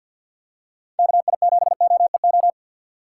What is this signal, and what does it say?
Morse code, heard as a keyed tone, about 52 wpm, 700 Hz. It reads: KI7OEO
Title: USA Call Signs - 6 Characters - 4x Character Spacing - 50wpm